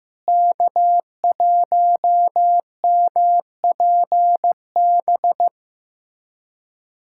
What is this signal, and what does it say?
Morse code, heard as a keyed tone, about 15 wpm, 700 Hz. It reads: K1MPB
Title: USA Callsign DE Callsign - 5 Characters - 15wpm